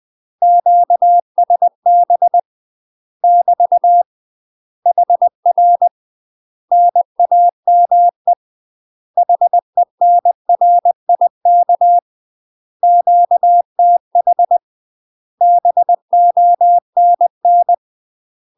Morse code, heard as a keyed tone, about 20 wpm, 700 Hz. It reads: QSB <BT> HR NAME HENRIK QTH BONN
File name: UK MASTER QSOs TRAINING / 20WPM / UK MASTER QSOs 0017